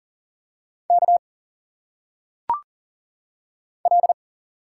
Morse code, heard as a keyed tone, about 40 wpm, 700 Hz. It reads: K L
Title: Single Letters - Phonetic Pronunciation 40wpm